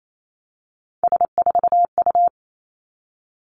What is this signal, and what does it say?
Morse code, heard as a keyed tone, about 28 wpm, 700 Hz. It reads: S4U
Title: Three Challenging Letter-Number Combinations 28wpm